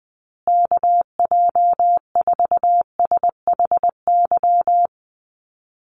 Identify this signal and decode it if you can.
Morse code, heard as a keyed tone, about 19 wpm, 700 Hz. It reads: KJ4SHY
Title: USA Call Signs Repeated 3x 20wpm